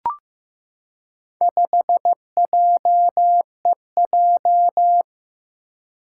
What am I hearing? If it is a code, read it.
Morse code, heard as a keyed tone, about 15 wpm, 700 Hz. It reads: 5JEJ
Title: Four Challenging Letter-Number Combinations 15wpm